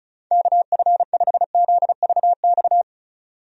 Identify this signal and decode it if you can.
Morse code, heard as a keyed tone, about 34 wpm, 700 Hz. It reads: KF5ZVX
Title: USA Call Signs - 6 Characters - 35wpm